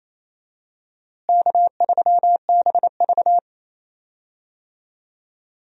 Morse code, heard as a keyed tone, about 28 wpm, 700 Hz. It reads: K3BV
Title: USA Call Signs - 4 Characters - Repeated 3x - 28wpm